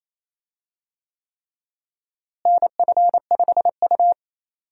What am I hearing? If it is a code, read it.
Morse code, heard as a keyed tone, about 28 wpm, 700 Hz. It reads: NF5U
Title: USA Call Signs - 4 Characters - Repeated 3x - 28wpm